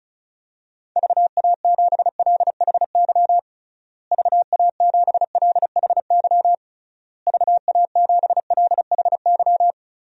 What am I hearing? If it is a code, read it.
Morse code, heard as a keyed tone, about 36 wpm, 700 Hz. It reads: VA7LHY VA7LHY VA7LHY
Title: DXCC-100 Call Signs - 6 Characters - Repeated 3x - 35wpm